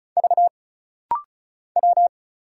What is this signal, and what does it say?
Morse code, heard as a keyed tone, about 36 wpm, 700 Hz. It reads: V W
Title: Single Letters - Rapid-Fire 35wpm